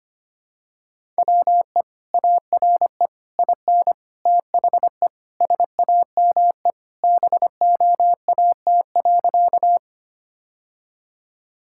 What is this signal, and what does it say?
Morse code, heard as a keyed tone, about 25 wpm, 700 Hz. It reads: WE ARE IN THE SAME BOAT.
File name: practice / 25wpm / Sentences with 3-Word Phrases - 25wpm